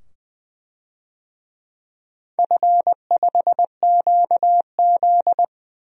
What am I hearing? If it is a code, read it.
Morse code, heard as a keyed tone, about 20 wpm, 700 Hz. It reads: F5QZ
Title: DXCC-100 Call Signs - 4 Characters - 20wpm